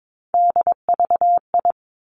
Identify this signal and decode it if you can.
Morse code, heard as a keyed tone, about 22 wpm, 700 Hz. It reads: DVI
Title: Three Challenging Letter-Number Combinations 22wpm